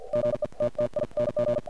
Morse code, heard as a keyed tone, about 48 wpm, spaced as for 41 wpm, 590 Hz. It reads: GETTING
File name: archived / CRSnet